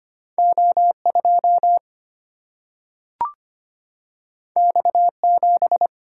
Morse code, heard as a keyed tone, about 25 wpm, 700 Hz. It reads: O 2 X 7
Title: Single-Letter Single-Number 25wpm